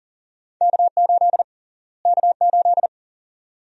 Morse code, heard as a keyed tone, about 40 wpm, 700 Hz. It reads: K8 K8
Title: USA Call Sign Prefixes - Repeated 2x - 40wpm